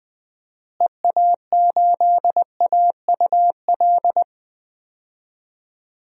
Morse code, heard as a keyed tone, about 20 wpm, 700 Hz. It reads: EA8AUL
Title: DXCC-100 Call Signs - 6 Characters - Repeated 3x - 20wpm